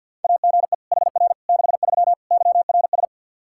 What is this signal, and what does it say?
Morse code, heard as a keyed tone, about 47 wpm, 700 Hz. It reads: AGE HR 63 YRS
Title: K1USN SST QSO Practice 50wpm